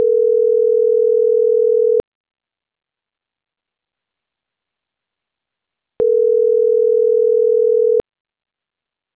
Ringback tone (North America), 440 + 480 Hz.